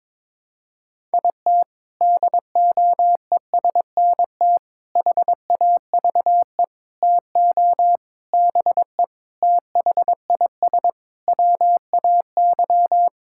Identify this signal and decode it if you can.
Morse code, heard as a keyed tone, about 22 wpm, 700 Hz. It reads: IT DOESNT HAVE TO BE THIS WAY
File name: practice / 22wpm / Sets of 7 Words - 22wpm